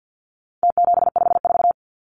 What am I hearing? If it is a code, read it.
Morse code, heard as a keyed tone, about 50 wpm, 700 Hz. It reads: T754
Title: Numbers - Contest Style - 4-Digit 50wpm